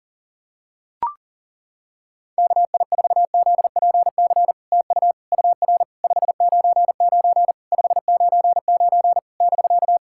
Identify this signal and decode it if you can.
Morse code, heard as a keyed tone, about 40 wpm, 700 Hz. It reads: KI4ZPC TU UR 599 599 <BK>
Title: SOTA QSO Practice 40wpm